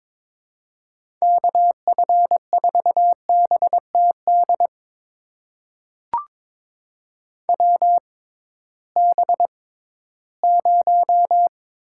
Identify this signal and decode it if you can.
Morse code, heard as a keyed tone, about 21 wpm, spaced as for 14 wpm, 700 Hz. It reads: KF4BTD WB0
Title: USA Call Signs - 6 Characters - 6x Character Spacing - 22wpm